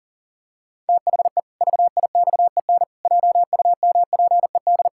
Morse code, heard as a keyed tone, about 37 wpm, 700 Hz. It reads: THE VIXEN JUMPED
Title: Pangram Challenge v2 - 40wpm